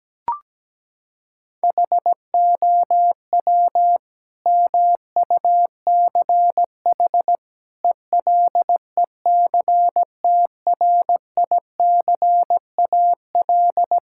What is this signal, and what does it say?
Morse code, heard as a keyed tone, about 17 wpm, 700 Hz. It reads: HOW MUCH ELECTRICAL
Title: Sentences from Top 1000 Words - Review Part 6 - 17wpm